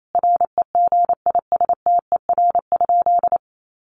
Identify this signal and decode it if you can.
Morse code, heard as a keyed tone, about 28 wpm, 700 Hz. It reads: REGISTER?
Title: Sentences from Top 1200 Words 28wpm